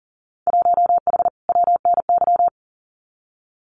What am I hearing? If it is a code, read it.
Morse code, heard as a keyed tone, about 41 wpm, 700 Hz. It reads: 1H WNY
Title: Winter Field Day Exchanges 40wpm